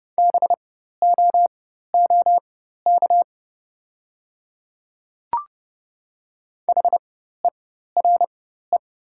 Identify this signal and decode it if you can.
Morse code, heard as a keyed tone, about 30 wpm, spaced as for 14 wpm, 700 Hz. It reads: BOOK HERE
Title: 4x Character Spacing - Top 300 Words - 30wpm